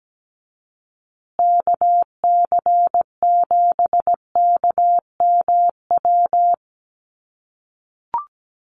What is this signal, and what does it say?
Morse code, heard as a keyed tone, about 17 wpm, 700 Hz. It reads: KC7KMW
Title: USA Call Signs Repeated 3x 17wpm